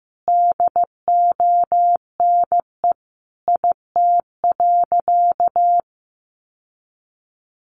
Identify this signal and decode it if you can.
Morse code, heard as a keyed tone, about 15 wpm, 700 Hz. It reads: DONE IT.
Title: Sentences from Top 1200 Words 15wpm